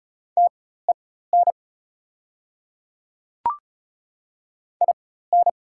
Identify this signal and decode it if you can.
Morse code, heard as a keyed tone, about 33 wpm, spaced as for 16 wpm, 690 Hz. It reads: TEN IN
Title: 4x Character Spacing - Top 500 Words - 35wpm